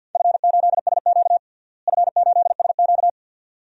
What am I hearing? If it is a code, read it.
Morse code, heard as a keyed tone, about 51 wpm, 700 Hz. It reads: F8SQ F8SQ
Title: DXCC-100 Call Signs - 4 Characters - Repeated 2x - 50wpm